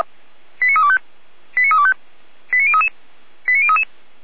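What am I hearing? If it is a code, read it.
Five-tone selective call (ZVEI), heard as quick runs of five tones, tones of about 70 ms. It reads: 89325 89325 89033 89033
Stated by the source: ZVEI five-tone selective call received by SDR, on the Feuerwehr Orlamünde fire brigade audio server.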